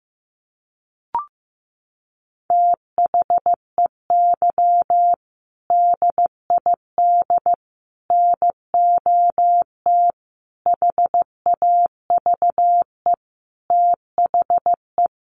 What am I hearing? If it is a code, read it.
Morse code, heard as a keyed tone, about 15 wpm, 700 Hz. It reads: THEY DID NOT HAVE THE
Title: Sentences from Top 1400 Words 15wpm